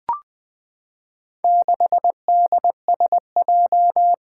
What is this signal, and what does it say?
Morse code, heard as a keyed tone, about 20 wpm, 700 Hz. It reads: 6DSJ
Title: Four Challenging Letter-Number Combinations 20wpm